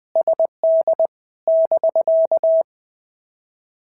Morse code, heard as a keyed tone, about 19 wpm, 650 Hz. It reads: SD <BK>
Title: POTA QSO Practice - 20wpm